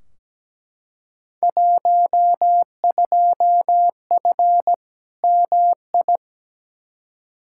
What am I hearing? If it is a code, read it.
Morse code, heard as a keyed tone, about 17 wpm, 700 Hz. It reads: 12F MI
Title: Field Day Exchanges Encore - 17wpm